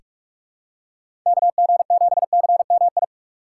Morse code, heard as a keyed tone, about 47 wpm, 700 Hz. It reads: KG7CMI
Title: USA Call Signs - 6 Characters - 6x Character Spacing - 45wpm